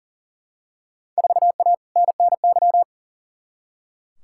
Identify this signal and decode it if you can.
Morse code, heard as a keyed tone, about 40 wpm, 700 Hz. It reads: 4A NNY
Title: Field Day Exchanges - 40wpm